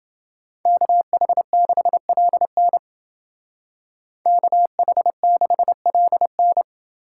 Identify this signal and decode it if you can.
Morse code, heard as a keyed tone, about 29 wpm, 700 Hz. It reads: KH6LN KH6LN